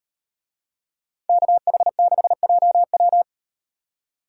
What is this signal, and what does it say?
Morse code, heard as a keyed tone, about 38 wpm, 700 Hz. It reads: KH6JW